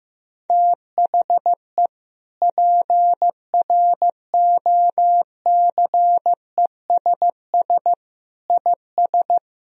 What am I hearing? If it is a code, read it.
Morse code, heard as a keyed tone, about 15 wpm, 700 Hz. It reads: THE PROCESS IS
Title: Sentences from Top 1000 Words - Review Part 3 - 15wpm